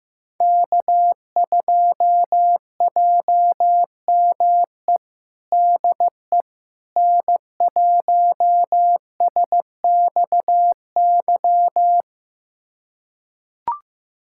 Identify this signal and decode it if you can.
Morse code, heard as a keyed tone, about 15 wpm, 700 Hz. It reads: K2JME DE N1SXY
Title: USA Callsign DE Callsign - 5 Characters - 15wpm